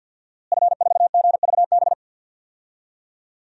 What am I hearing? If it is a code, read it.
Morse code, heard as a keyed tone, about 48 wpm, 700 Hz. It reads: F4GVB
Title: DXCC-100 Call Signs - 5 Characters - Repeated 3x - 50wpm